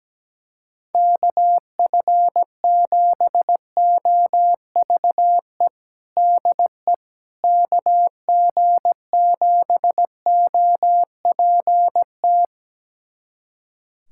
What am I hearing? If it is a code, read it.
Morse code, heard as a keyed tone, about 17 wpm, 700 Hz. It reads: KF7OVE DE KG7OPT
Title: USA Callsign DE Callsign - 6 Characters - 17wpm